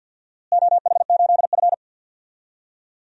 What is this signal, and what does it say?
Morse code, heard as a keyed tone, about 50 wpm, 700 Hz. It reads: OH8F
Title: DXCC-100 Call Signs - 4 Characters - Repeated 3x - 50wpm